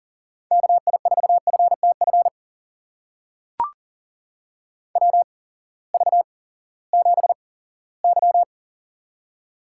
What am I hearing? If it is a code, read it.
Morse code, heard as a keyed tone, about 40 wpm, spaced as for 20 wpm, 700 Hz. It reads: KI4FTF WV7Y
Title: USA Call Signs - 8x Character Spacing - 40wpm